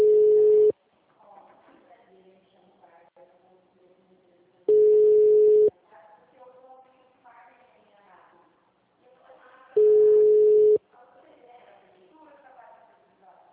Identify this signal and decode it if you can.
Ringback tone (Europe, 425 Hz), with speech.